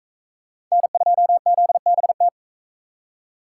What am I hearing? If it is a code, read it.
Morse code, heard as a keyed tone, about 38 wpm, 700 Hz. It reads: N2ZBT